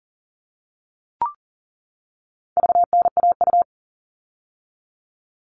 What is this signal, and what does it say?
Morse code, heard as a keyed tone, about 40 wpm, 700 Hz. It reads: VNAU
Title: All Cut Numbers - 4-Digit 40wpm